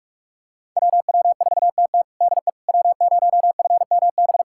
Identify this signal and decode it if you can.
Morse code, heard as a keyed tone, about 42 wpm, 700 Hz. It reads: WW4TT DE W0FMB
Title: USA Callsign DE Callsign - 5 Characters - 45wpm